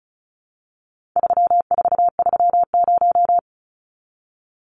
Morse code, heard as a keyed tone, about 34 wpm, 700 Hz. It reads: 3430
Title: Numbers - 4-Digit 35wpm